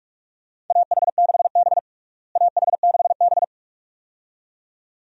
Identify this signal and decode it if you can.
Morse code, heard as a keyed tone, about 41 wpm, 700 Hz. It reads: AH6B AH6B
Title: USA Call Signs - 4 Characters - Repeated 2x - 45wpm